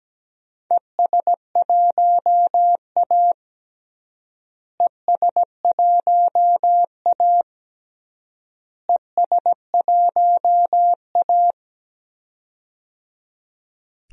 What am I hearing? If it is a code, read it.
Morse code, heard as a keyed tone, about 17 wpm, 700 Hz. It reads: ES1A ES1A ES1A
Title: DXCC-100 Call Signs - 4 Characters - Repeated 3x - 17wpm